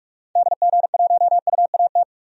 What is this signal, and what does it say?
Morse code, heard as a keyed tone, about 42 wpm, 700 Hz. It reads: DG1UAT
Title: DXCC-100 Call Signs - 6 Characters - 45wpm